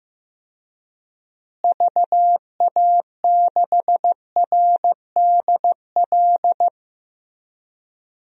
Morse code, heard as a keyed tone, about 15 wpm, 700 Hz. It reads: VA6RDL
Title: DXCC-100 Call Signs - 6 Characters - Repeated 3x - 15wpm